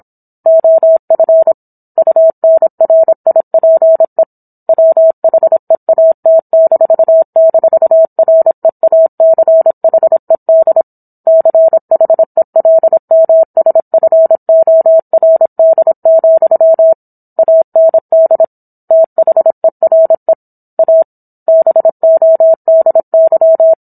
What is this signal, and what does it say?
Morse code, heard as a keyed tone, about 27 wpm, 650 Hz. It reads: OF UNRIPE WHEAT--REACHED CHELMSFORD, AND THERE A BODY
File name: Book One / 17-Chapter 17 - 26 WPM